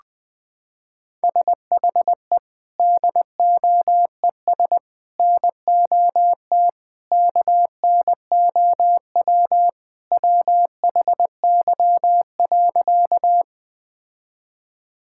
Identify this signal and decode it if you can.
Morse code, heard as a keyed tone, about 21 wpm, 700 Hz. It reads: SHE DOES NOT KNOW WHY.
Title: Sentences with 4-Word Phrases - 20wpm